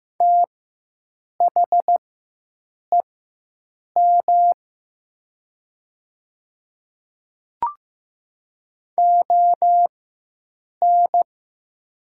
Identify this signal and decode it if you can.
Morse code, heard as a keyed tone, about 15 wpm, spaced as for 7 wpm, 700 Hz. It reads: THEM ON
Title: 4x Character Spacing - Top 100 Words - 15wpm